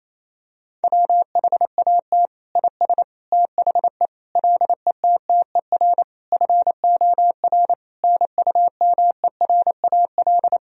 Morse code, heard as a keyed tone, about 27 wpm, 700 Hz. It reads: WHAT IS THE LETTER FOR NUMERAL
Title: Sentences from Top 1000 Words - Review Part 2 - 28wpm